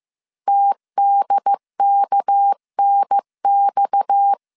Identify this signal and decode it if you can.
Morse code, heard as a keyed tone, about 15 wpm, 800 Hz. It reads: TDKNX
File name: CW / DET / 1 6 1500C S C 14wpm 2951